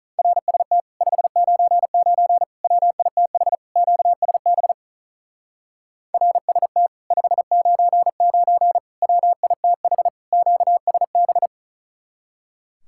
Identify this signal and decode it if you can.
Morse code, heard as a keyed tone, about 44 wpm, 700 Hz. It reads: RST 599 WITH QSB RST 599 WITH QSB